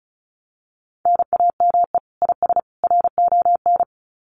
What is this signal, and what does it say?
Morse code, heard as a keyed tone, about 34 wpm, 700 Hz. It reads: NAME IS RON
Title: Common QSO Phrases 35wpm